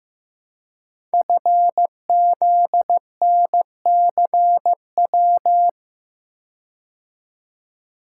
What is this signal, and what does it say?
Morse code, heard as a keyed tone, about 15 wpm, 700 Hz. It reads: FZNCW
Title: Code Groups - 5 Letters - 15wpm